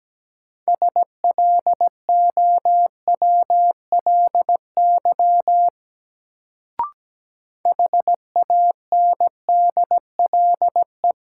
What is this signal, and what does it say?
Morse code, heard as a keyed tone, about 17 wpm, 700 Hz. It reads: SLOWLY HANDLE
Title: Sentences from Top 1100 Words - New Word Drill Alt - 17wpm